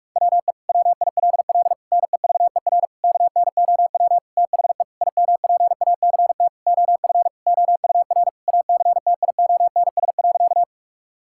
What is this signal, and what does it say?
Morse code, heard as a keyed tone, about 46 wpm, 700 Hz. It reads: WE WILL NEVER KNOW THE IMPACT OF OUR ACTIONS.
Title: Sentences from Top 1000 Words - Review Part 6 - 45wpm